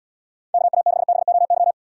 Morse code, heard as a keyed tone, about 78 wpm, 700 Hz. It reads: KA6BOY